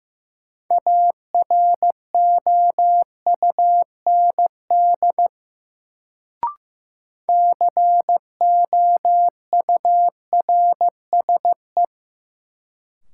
Morse code, heard as a keyed tone, about 15 wpm, 700 Hz. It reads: AROUND COURSE
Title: Sets of 2 Words - New Word Drill - 15wpm